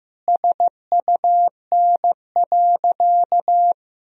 Morse code, heard as a keyed tone, about 15 wpm, 700 Hz. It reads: SUN.
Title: Sentences from Top 1000 Words - Review Part 1 - 15wpm